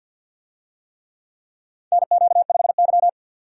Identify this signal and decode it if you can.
Morse code, heard as a keyed tone, about 50 wpm, 700 Hz. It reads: NQ5Y
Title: USA Call Signs - 4 Characters - 50wpm